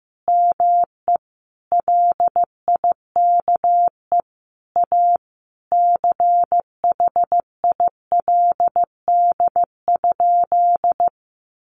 Morse code, heard as a keyed tone, about 15 wpm, 700 Hz. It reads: ME LIKE A CHILD?
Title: Sentences from Top 1300 Words 15wpm